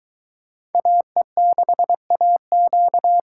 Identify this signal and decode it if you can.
Morse code, heard as a keyed tone, about 22 wpm, 700 Hz. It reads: AE6AQ